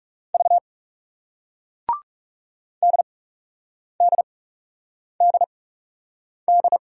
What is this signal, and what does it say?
Morse code, heard as a keyed tone, about 40 wpm, spaced as for 8 wpm, 700 Hz. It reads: V DDDD